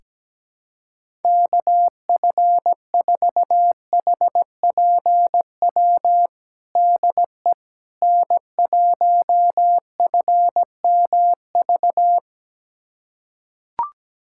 Morse code, heard as a keyed tone, about 17 wpm, 700 Hz. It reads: KF4HPW DE N1FMV
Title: USA Callsign DE Callsign 17wpm